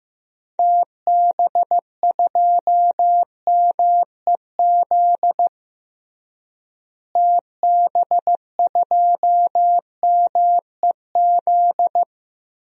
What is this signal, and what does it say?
Morse code, heard as a keyed tone, about 15 wpm, 700 Hz. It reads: TB2MEZ TB2MEZ
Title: DXCC-100 Call Signs - 6 Characters - Repeated 3x - 15wpm